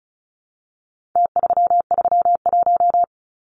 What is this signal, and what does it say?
Morse code, heard as a keyed tone, about 34 wpm, 700 Hz. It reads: T331
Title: Numbers - Contest Style - 4-Digit 35wpm